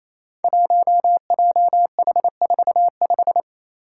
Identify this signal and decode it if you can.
Morse code, heard as a keyed tone, about 28 wpm, 700 Hz. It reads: 1JH45
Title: Five Challenging Letter-Number Combinations 28wpm